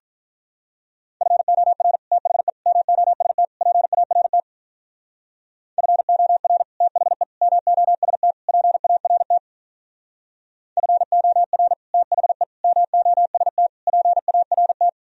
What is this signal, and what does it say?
Morse code, heard as a keyed tone, about 44 wpm, 700 Hz. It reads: FOR THE MOST PART FOR THE MOST PART FOR THE MOST PART